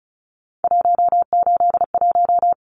Morse code, heard as a keyed tone, about 34 wpm, 700 Hz. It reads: 181
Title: Numbers - 3-Digit 35wpm